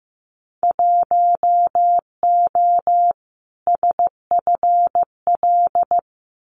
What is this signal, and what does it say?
Morse code, heard as a keyed tone, about 15 wpm, 700 Hz. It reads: 1O SFL
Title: Winter Field Day Exchanges 15wpm